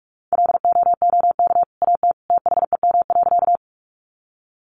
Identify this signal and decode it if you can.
Morse code, heard as a keyed tone, about 44 wpm, 700 Hz. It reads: LOOK AT THEM.
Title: Sentences from Top 100 Words 45wpm